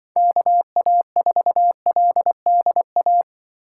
Morse code, heard as a keyed tone, about 24 wpm, 700 Hz. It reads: KA4LDA
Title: Speed Racing - USA 2x3 Call Signs 20wpm